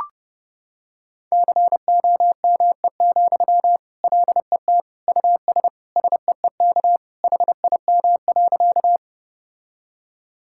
Morse code, heard as a keyed tone, about 30 wpm, 700 Hz. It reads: COME, LET US SEEK HIM.
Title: Sentences from Top 1600 Words 30wpm